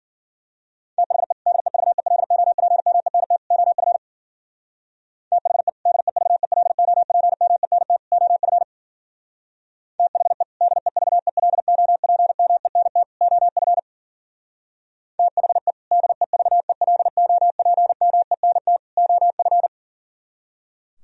Morse code, heard as a keyed tone, about 52 wpm, 700 Hz. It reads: THE DEVELOPMENT OF THE DEVELOPMENT OF THE DEVELOPMENT OF THE DEVELOPMENT OF